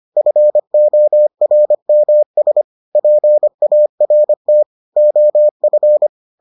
Morse code, heard as a keyed tone, about 25 wpm, 600 Hz. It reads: FORMS PART OF